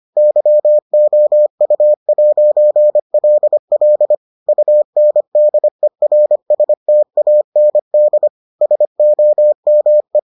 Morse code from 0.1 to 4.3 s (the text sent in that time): YOU'LL